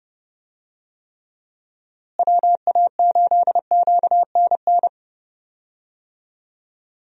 Morse code, heard as a keyed tone, about 30 wpm, 700 Hz. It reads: WA8QNN
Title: USA Call Signs - 6 Characters - Repeated 3x - 30wpm